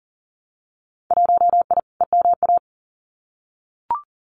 Morse code, heard as a keyed tone, about 40 wpm, 700 Hz. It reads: 1I EMA
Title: Winter Field Day Exchanges Encore 40wpm